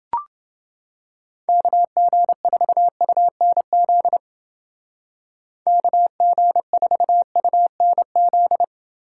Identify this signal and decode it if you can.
Morse code, heard as a keyed tone, about 29 wpm, 700 Hz. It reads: KG4UNZ KG4UNZ